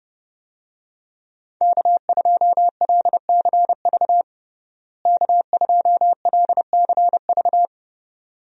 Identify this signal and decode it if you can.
Morse code, heard as a keyed tone, about 30 wpm, 700 Hz. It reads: K2LCV K2LCV
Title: USA Call Signs - 5 Characters - Repeated 2x - 30wpm